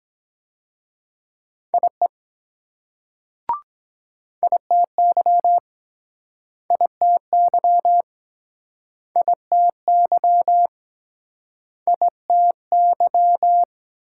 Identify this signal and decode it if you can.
Morse code, heard as a keyed tone, about 23 wpm, 700 Hz. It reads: IE ITY ITY ITY ITY